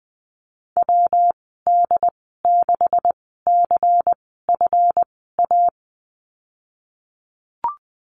Morse code, heard as a keyed tone, about 20 wpm, spaced as for 15 wpm, 700 Hz. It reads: WD6CFA
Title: 154 Lesson 25g - Letter p - Call Signs 20wpm